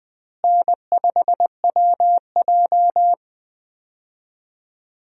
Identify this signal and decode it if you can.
Morse code, heard as a keyed tone, about 20 wpm, 700 Hz. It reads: N5WJ
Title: USA Call Signs - 4 Characters - Repeated 2x - 20wpm